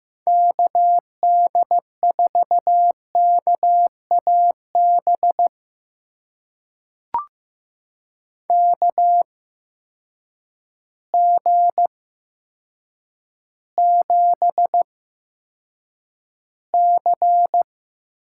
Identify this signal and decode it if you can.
Morse code, heard as a keyed tone, about 15 wpm, spaced as for 8 wpm, 700 Hz. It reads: KD4KAB KG7C
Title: USA Call Signs - 6 Characters - 8x Character Spacing - 15wpm